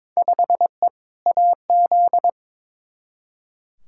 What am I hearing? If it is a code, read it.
Morse code, heard as a keyed tone, about 22 wpm, 700 Hz. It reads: 5E AZ
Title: Field Day Exchanges - 22wpm